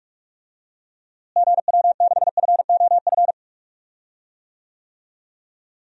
Morse code, heard as a keyed tone, about 46 wpm, 700 Hz. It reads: GW6FOF
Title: DXCC-100 Call Signs - 6 Characters - Repeated 2x - 45wpm